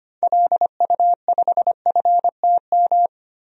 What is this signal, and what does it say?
Morse code, heard as a keyed tone, about 25 wpm, 700 Hz. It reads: LU5FTM
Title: DXCC-100 Call Signs - 6 Characters - Repeated 2x - 25wpm